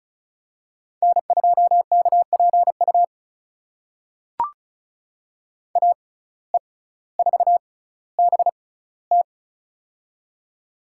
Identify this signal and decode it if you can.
Morse code, heard as a keyed tone, about 36 wpm, spaced as for 18 wpm, 700 Hz. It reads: N2KPU AE4BT
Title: USA Call Signs - 5 Characters - 6x Character Spacing - 35wpm